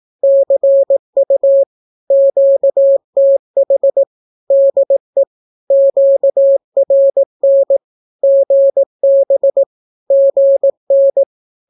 Morse code, heard as a keyed tone, about 18 wpm, 550 Hz. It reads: CU QTH DE QRN GB GN
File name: naukapodstaw / skroty / skroty